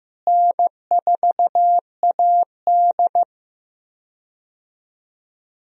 Morse code, heard as a keyed tone, about 15 wpm, 700 Hz. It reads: N4AD
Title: USA Call Signs - 4 Characters - Repeated 3x - 15wpm